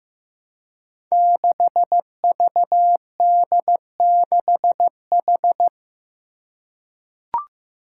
Morse code, heard as a keyed tone, about 15 wpm, 700 Hz. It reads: 6VD6H
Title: Five Challenging Letter-Number Combinations 15wpm